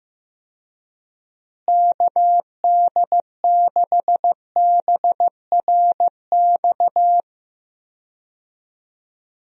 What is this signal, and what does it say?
Morse code, heard as a keyed tone, about 15 wpm, 700 Hz. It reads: KD6BRX
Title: USA Call Signs - 6 Characters - Repeated 3x - 15wpm